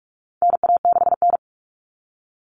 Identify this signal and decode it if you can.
Morse code, heard as a keyed tone, about 44 wpm, 700 Hz. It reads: NA6N
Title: Common Cut Numbers - 4-Digit 45wpm